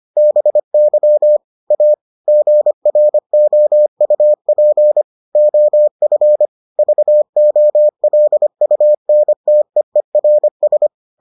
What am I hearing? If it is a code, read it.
Morse code, heard as a keyed tone, about 25 wpm, 600 Hz. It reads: BY A GROUP OF VOLUNTEERS